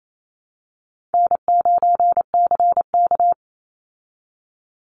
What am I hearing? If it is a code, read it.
Morse code, heard as a keyed tone, about 28 wpm, 700 Hz. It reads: N9CK
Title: Easy Call Signs Repeated 2x - 28wpm